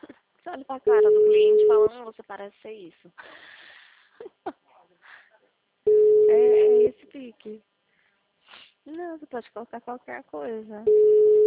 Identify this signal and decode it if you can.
Ringback tone (Europe, 425 Hz), with speech.
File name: out- -2003- - - .7200